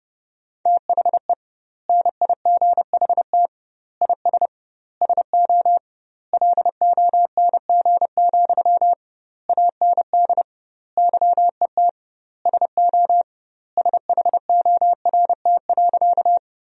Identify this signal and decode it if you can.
Morse code, heard as a keyed tone, about 29 wpm, 700 Hz. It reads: THE NIGHT IS SO LONG, AND YET SO SHORT.
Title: 2x Word-Spacing - Sentences from Top 500 Words - 30wpm